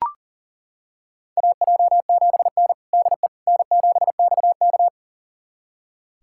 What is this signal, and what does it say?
Morse code, heard as a keyed tone, about 40 wpm, 700 Hz. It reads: AJ7N DE N7XK
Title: USA Callsign DE Callsign - 4 Characters - 40wpm